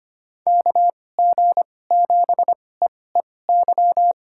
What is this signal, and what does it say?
Morse code, heard as a keyed tone, about 25 wpm, spaced as for 18 wpm, 700 Hz. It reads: KG7EEY
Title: USA Call Signs - 2x Character Spacing - 25wpm